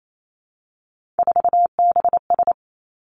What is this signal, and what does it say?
Morse code, heard as a keyed tone, about 28 wpm, 690 Hz. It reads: 4BS
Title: Three Challenging Letter-Number Combinations 28wpm